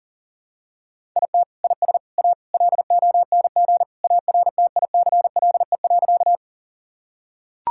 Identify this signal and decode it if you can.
Morse code, heard as a keyed tone, about 40 wpm, 700 Hz. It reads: IT IS A LONG ARTICLE.
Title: Sentences from Top 1000 Words - Review Part 5 - 40wpm